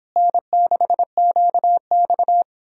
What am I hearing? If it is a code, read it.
Morse code, heard as a keyed tone, about 31 wpm, 700 Hz. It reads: N6QX